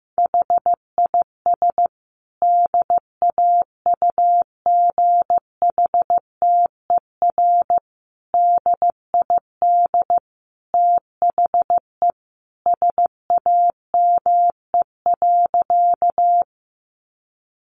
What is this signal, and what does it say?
Morse code, heard as a keyed tone, about 15 wpm, 700 Hz. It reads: HIS DAUGHTER DID THE SAME.
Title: Sentences from Top 1500 Words 15wpm